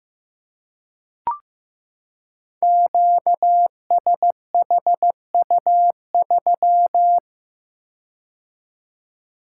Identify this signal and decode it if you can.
Morse code, heard as a keyed tone, about 15 wpm, 700 Hz. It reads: QSHU3
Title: Code Groups - 5 Letter:Numbers - 15wpm